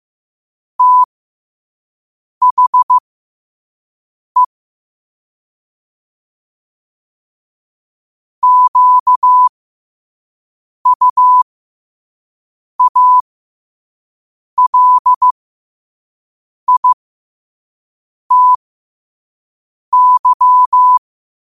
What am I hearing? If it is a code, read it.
Morse code, heard as a keyed tone, about 15 wpm, spaced as for 5 wpm, 1000 Hz. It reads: THE QUALITY
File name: qotdfiles / 5wpm